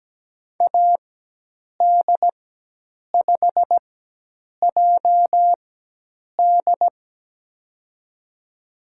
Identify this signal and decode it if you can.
Morse code, heard as a keyed tone, about 17 wpm, spaced as for 8 wpm, 700 Hz. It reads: AD5JD
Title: USA Call Signs - 4x Character Spacing - 17wpm